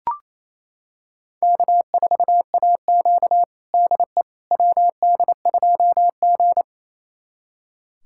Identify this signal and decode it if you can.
Morse code, heard as a keyed tone, about 28 wpm, 700 Hz. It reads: K4AQ DE WD2G
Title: USA Callsign DE Callsign - 4 Characters - 28wpm